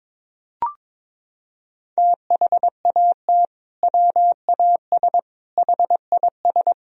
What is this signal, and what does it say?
Morse code, heard as a keyed tone, about 21 wpm, 700 Hz. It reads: THAT WAS HIS
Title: Sentences from Top 1000 Words - Review Part 2 - 22wpm